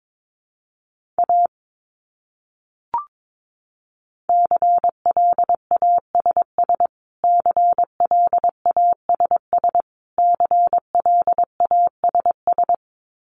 Morse code, heard as a keyed tone, about 22 wpm, 700 Hz. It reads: A CLASS CLASS CLASS
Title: Top 500 Words Repeated 22wpm